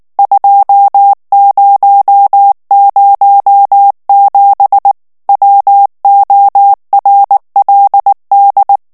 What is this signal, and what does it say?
Morse code, heard as a keyed tone, about 19 wpm, 800 Hz. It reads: 2007 WORLD
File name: ANGLAIS texte clair 3 vitesse 18 mots minute